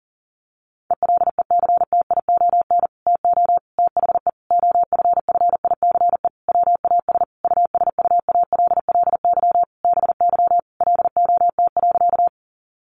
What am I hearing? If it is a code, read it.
Morse code, heard as a keyed tone, about 37 wpm, 700 Hz. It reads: ELECTION TO THE OFFICE WAS USUALLY BY LOT.
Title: Sentences from Top 1500 Words 40wpm